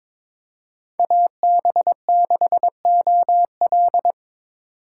Morse code, heard as a keyed tone, about 25 wpm, 700 Hz. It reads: AB6OL
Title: Speed Racing - USA 2x2 Call Signs 22wpm